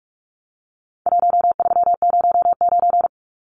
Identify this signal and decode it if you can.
Morse code, heard as a keyed tone, about 44 wpm, 700 Hz. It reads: 1309
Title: Numbers - 4-Digit 45wpm